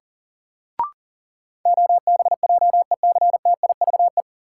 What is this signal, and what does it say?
Morse code, heard as a keyed tone, about 40 wpm, 700 Hz. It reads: OBJECTIVE
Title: Sentences from Top 1200 Words - New Word Drill - 40wpm